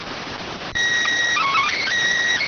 SSTV (slow-scan television), PD 290.